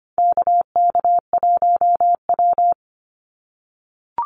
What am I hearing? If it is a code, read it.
Morse code, heard as a keyed tone, about 25 wpm, 700 Hz. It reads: KK1W
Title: NA SOTA Activator Callsigns 25wpm